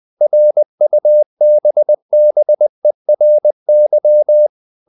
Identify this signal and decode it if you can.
Morse code, heard as a keyed tone, about 20 wpm, 600 Hz. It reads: RUBBERY
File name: the old man and the sea part 2 20wpm